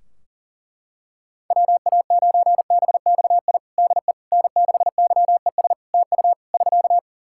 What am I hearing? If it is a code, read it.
Morse code, heard as a keyed tone, about 40 wpm, 700 Hz. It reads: WA9BXI DE N6YES TU <SK>
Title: Practice QSO - Exchange 4 - Ending - 40wpm